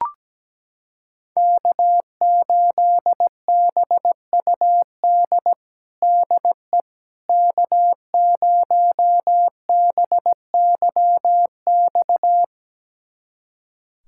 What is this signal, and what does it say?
Morse code, heard as a keyed tone, about 17 wpm, 700 Hz. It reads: K8BUD DE K0BYX
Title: USA Callsign DE Callsign - 5 Characters - 17wpm